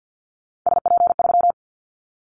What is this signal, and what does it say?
Morse code, heard as a keyed tone, about 50 wpm, 700 Hz. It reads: SP3
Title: Callsign Prefixes for Top 25 DXCC - 50wpm